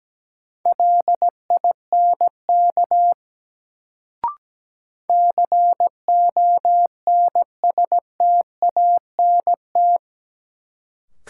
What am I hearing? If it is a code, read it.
Morse code, heard as a keyed tone, about 17 wpm, 700 Hz. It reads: LINK CONSTANT
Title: Sentences from Top 1500 Words - New Word Drill Alt - 17wpm